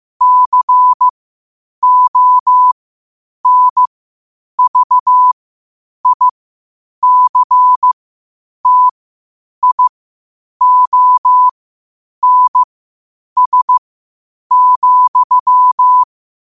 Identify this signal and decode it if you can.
Morse code, heard as a keyed tone, about 15 wpm, spaced as for 8 wpm, 1000 Hz. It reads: CONVICTIONS,
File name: qotdfiles / 8wpm